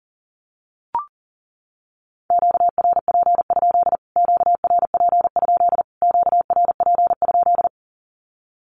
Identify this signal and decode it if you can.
Morse code, heard as a keyed tone, about 40 wpm, 700 Hz. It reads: QRP? QRP? QRP?
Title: Q-Codes Repeated 3x 40wpm